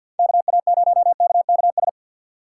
Morse code, heard as a keyed tone, about 53 wpm, 700 Hz. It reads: KA0KKS
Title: USA Call Signs - 6 Characters - 8x Character Spacing - 50wpm